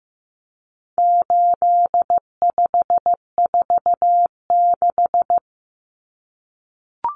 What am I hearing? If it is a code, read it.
Morse code, heard as a keyed tone, about 15 wpm, 700 Hz. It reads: 8546
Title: Common Cut Numbers - 4-Digit 15wpm